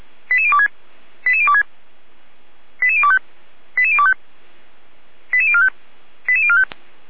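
Five-tone selective call (ZVEI), heard as quick runs of five tones, tones of about 70 ms. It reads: 88026 88026 88025 88025 88045 88045
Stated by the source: Five-tone ZVEI selective call received on SDR, code 88109.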